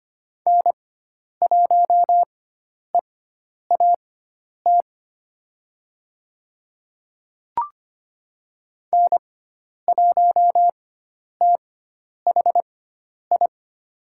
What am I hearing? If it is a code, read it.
Morse code, heard as a keyed tone, about 25 wpm, spaced as for 10 wpm, 700 Hz. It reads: N1EAT N1THI
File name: Lesson 16g - Letter c - Call Signs 25wpm